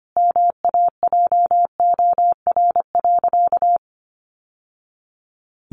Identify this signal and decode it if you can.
Morse code, heard as a keyed tone, about 25 wpm, 700 Hz. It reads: MAJOR.
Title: Sentences from Top 600 Words 25wpm